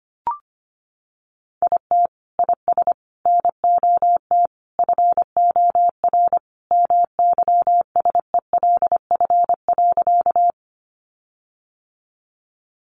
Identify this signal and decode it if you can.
Morse code, heard as a keyed tone, about 25 wpm, 700 Hz. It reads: IT IS NOT FOR MYSELF.
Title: Sentences from Top 1300 Words 25wpm